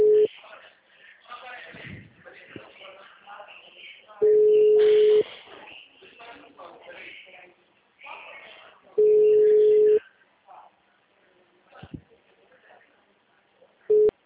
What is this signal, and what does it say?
Ringback tone (Europe, 425 Hz), with speech.
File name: out- -2008- - - .604